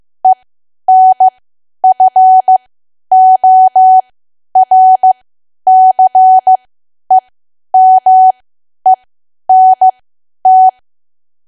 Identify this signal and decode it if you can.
Morse code, heard as a keyed tone, about 15 wpm, spaced as for 10 wpm, 750 Hz. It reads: ENFORCEMENT
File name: cwa / W1AW-Level2 / 10WPM